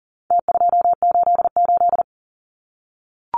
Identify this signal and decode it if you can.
Morse code, heard as a keyed tone, about 40 wpm, 700 Hz. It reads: T288
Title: Numbers - Contest Style - 4-Digit 40wpm